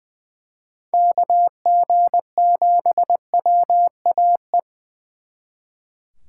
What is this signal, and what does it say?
Morse code, heard as a keyed tone, about 25 wpm, 700 Hz. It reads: KG7WAE
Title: Speed Racing - USA 2x3 Call Signs 20wpm